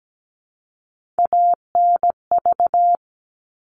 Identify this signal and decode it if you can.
Morse code, heard as a keyed tone, about 17 wpm, 700 Hz. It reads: ANV
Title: All Cut Numbers - 3-Digit 17wpm